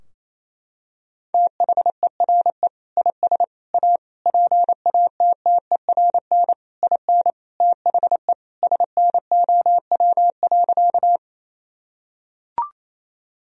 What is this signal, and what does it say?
Morse code, heard as a keyed tone, about 28 wpm, 700 Hz. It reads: THERE IS A PATTERN IN THE SNOW.
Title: Sentences from Top 1000 Words - Review Part 2 - 28wpm